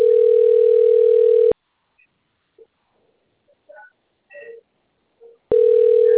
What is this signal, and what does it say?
Ringback tone (North America), 440 + 480 Hz.